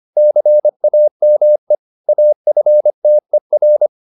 Morse code, heard as a keyed tone, about 25 wpm, 600 Hz. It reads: CAME AFTER
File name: the old man and the sea part 1 25wpm